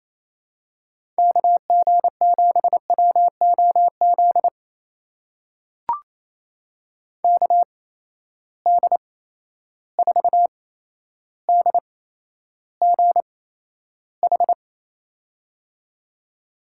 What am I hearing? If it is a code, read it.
Morse code, heard as a keyed tone, about 28 wpm, spaced as for 12 wpm, 700 Hz. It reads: KG7WOZ KD4DGH
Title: USA Call Signs - 6 Characters - 8x Character Spacing - 28wpm